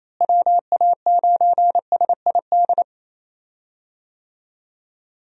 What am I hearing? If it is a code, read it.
Morse code, heard as a keyed tone, about 28 wpm, 700 Hz. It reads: WA9SID
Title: USA Call Signs - 6 Characters - Repeated 2x - 28wpm